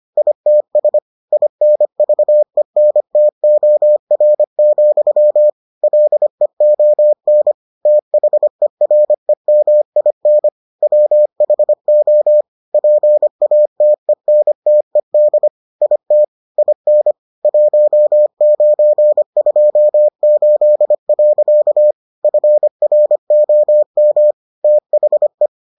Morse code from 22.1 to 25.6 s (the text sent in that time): FROM THE